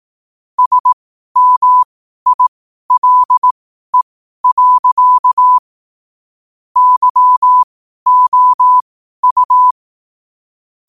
Morse code, heard as a keyed tone, about 18 wpm, spaced as for 12 wpm, 1000 Hz. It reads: SMILE. YOU
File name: qotdfiles / 12wpm